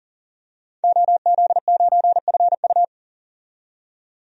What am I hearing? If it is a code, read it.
Morse code, heard as a keyed tone, about 38 wpm, 700 Hz. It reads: OZ9FU
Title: DXCC-100 Call Signs - 5 Characters - 40wpm